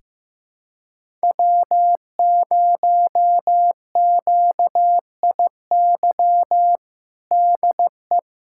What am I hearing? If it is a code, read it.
Morse code, heard as a keyed tone, about 15 wpm, 700 Hz. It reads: W0QIY DE
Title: USA Callsign DE Callsign - 5 Characters - 15wpm